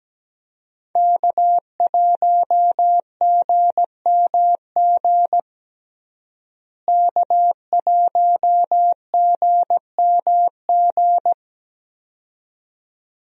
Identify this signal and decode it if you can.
Morse code, heard as a keyed tone, about 17 wpm, 700 Hz. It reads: K1GMG K1GMG
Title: USA Call Signs - 5 Characters - Repeated 2x - 17wpm